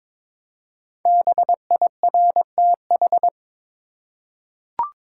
Morse code, heard as a keyed tone, about 22 wpm, 700 Hz. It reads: BIRTH
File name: 5 Letter Words - First 3 Letters Given - 22wpm